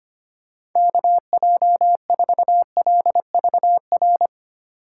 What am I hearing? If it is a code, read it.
Morse code, heard as a keyed tone, about 25 wpm, 700 Hz. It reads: KJ4LVR
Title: USA Call Signs - 8x Character Spacing - 25wpm